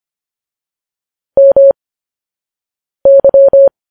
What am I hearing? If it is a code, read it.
Morse code, heard as a keyed tone, about 25 wpm, spaced as for 6 wpm, 550 Hz. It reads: MY